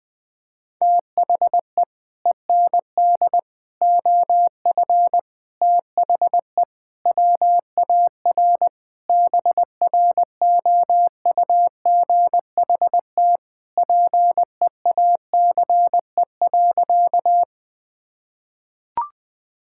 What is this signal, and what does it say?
Morse code, heard as a keyed tone, about 20 wpm, 700 Hz. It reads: THE END OF THE WAR BROUGHT PEACE.
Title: Sentences with 5-Word Phrases - 20wpm